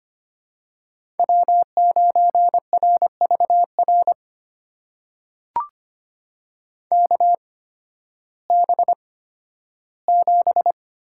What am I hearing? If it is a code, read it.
Morse code, heard as a keyed tone, about 25 wpm, spaced as for 13 wpm, 700 Hz. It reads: W9RVR KB7
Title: USA Call Signs - 5 Characters - 8x Character Spacing - 25wpm